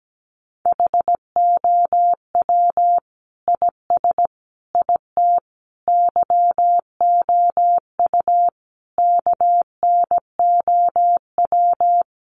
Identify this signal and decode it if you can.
Morse code, heard as a keyed tone, about 17 wpm, 700 Hz. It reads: HOW IS IT YOU KNOW
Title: Sentences from Top 700 Words 17wpm